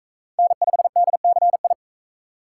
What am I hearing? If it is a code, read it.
Morse code, heard as a keyed tone, about 41 wpm, 700 Hz. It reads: N5DCI